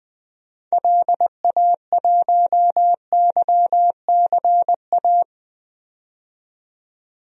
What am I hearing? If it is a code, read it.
Morse code, heard as a keyed tone, about 21 wpm, 700 Hz. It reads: LA1YCA
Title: DXCC-100 Call Signs - 6 Characters - 20wpm